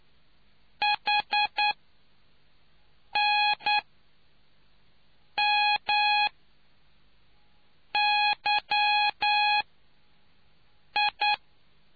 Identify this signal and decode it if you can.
Morse code, heard as a keyed tone, about 9 wpm, spaced as for 4 wpm, 830 Hz. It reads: HNMYI